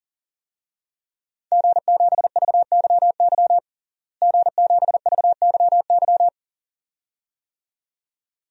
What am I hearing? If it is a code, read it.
Morse code, heard as a keyed tone, about 37 wpm, 700 Hz. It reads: G7VYY G7VYY
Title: DXCC-100 Call Signs - 5 Characters - Repeated 2x - 40wpm